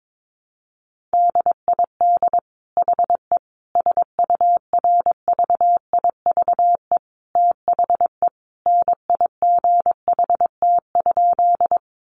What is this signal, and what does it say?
Morse code, heard as a keyed tone, about 22 wpm, 700 Hz. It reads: DID HE SURVIVE THE NIGHT?
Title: Sentences from Top 1300 Words 22wpm